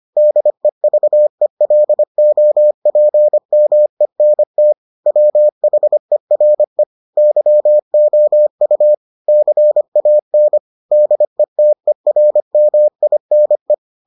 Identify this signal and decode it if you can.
Morse code, heard as a keyed tone, about 25 wpm, 600 Hz. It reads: DEVELOPMENT WHERE YOU CAN DETERMINE